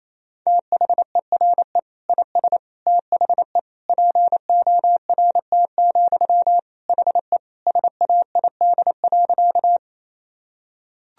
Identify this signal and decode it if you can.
Morse code, heard as a keyed tone, about 28 wpm, 700 Hz. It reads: THERE IS THE PORT, HE SAID.
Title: Sentences from Top 1000 Words - Review Part 1 - 28wpm